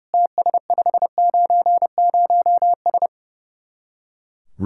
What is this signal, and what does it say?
Morse code, heard as a keyed tone, about 37 wpm, 700 Hz. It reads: TS590S
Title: Speed-Racing Common QSO Phrases Encore - 30wpm